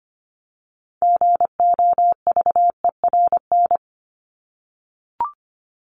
Morse code, heard as a keyed tone, about 25 wpm, 700 Hz. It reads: GOVERN
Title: Top 500 Words 25wpm